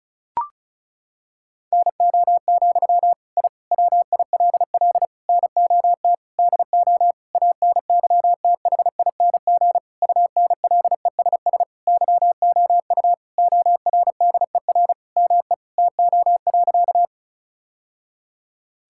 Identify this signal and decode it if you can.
Morse code, heard as a keyed tone, about 36 wpm, 700 Hz. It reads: NO, I WILL NOT DO ANYTHING UNLESS YOU ORDER ME TO.
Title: Sentences from Top 1000 Words - Review Part 6 - 35wpm